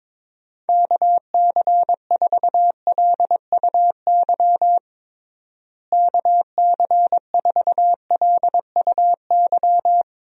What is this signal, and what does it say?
Morse code, heard as a keyed tone, about 21 wpm, 700 Hz. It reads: KC4LUY KC4LUY
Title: USA Call Signs - 6 Characters - Repeated 3x - 22wpm